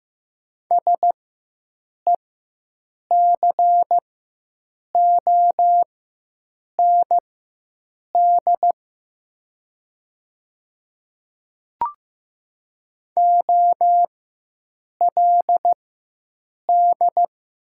Morse code, heard as a keyed tone, about 15 wpm, spaced as for 7 wpm, 700 Hz. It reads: SECOND OLD
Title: 4x Character Spacing - Top 500 Words - 15wpm